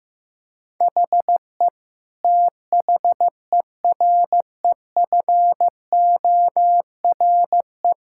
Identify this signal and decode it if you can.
Morse code, heard as a keyed tone, about 15 wpm, 700 Hz. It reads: HE THEREFORE
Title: Sentences from Top 1000 Words - Review Part 5 - 15wpm